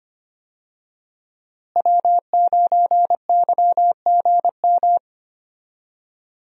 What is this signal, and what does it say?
Morse code, heard as a keyed tone, about 25 wpm, 700 Hz. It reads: W9YGM
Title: USA Call Signs - 5 Characters - Repeated 3x - 25wpm